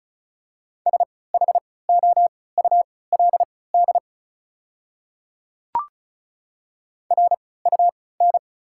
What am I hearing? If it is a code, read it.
Morse code, heard as a keyed tone, about 34 wpm, spaced as for 20 wpm, 690 Hz. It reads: SHOULD RUN
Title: 3x Character Spacing - Top 300 Words - 35wpm